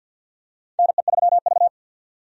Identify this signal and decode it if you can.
Morse code, heard as a keyed tone, about 50 wpm, 700 Hz. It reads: NE3V
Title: USA Call Signs - 4 Characters - 50wpm